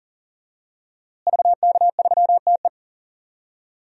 Morse code, heard as a keyed tone, about 38 wpm, 700 Hz. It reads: VK3TE
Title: DXCC-100 Call Signs - 5 Characters - Repeated 3x - 40wpm